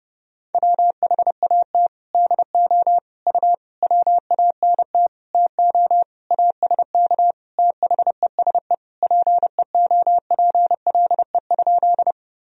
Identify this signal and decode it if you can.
Morse code, heard as a keyed tone, about 30 wpm, 700 Hz. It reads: WHAT DO U WANT TO ASK THESE PEOPLE?
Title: Sentences from Top 200 Words and Abbreviations 30wpm